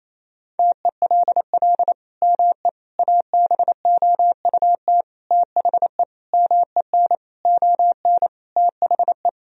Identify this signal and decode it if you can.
Morse code, heard as a keyed tone, about 28 wpm, 700 Hz. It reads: TELL ME ABOUT THE MEN ON THE
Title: Sentences from Top 1000 Words - Review Part 2 - 28wpm